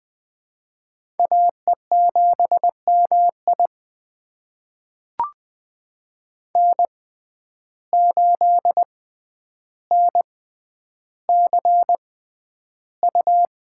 Morse code, heard as a keyed tone, about 20 wpm, spaced as for 10 wpm, 700 Hz. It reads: AE7MI N8NCU
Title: USA Call Signs - 5 Characters - 6x Character Spacing - 20wpm